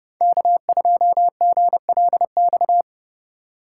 Morse code, heard as a keyed tone, about 32 wpm, 700 Hz. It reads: K2GLX
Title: USA Call Signs - 5 Characters - 4x Character Spacing - 30wpm